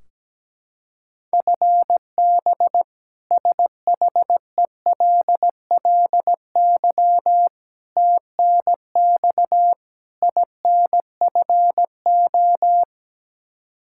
Morse code, heard as a keyed tone, about 17 wpm, 700 Hz. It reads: FB SHELLY TNX INFO
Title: K1USN SST QSO Practice 17wpm